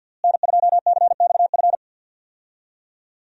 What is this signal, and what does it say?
Morse code, heard as a keyed tone, about 52 wpm, 700 Hz. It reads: N2CXF
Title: USA Call Signs - 5 Characters - Repeated 3x - 50wpm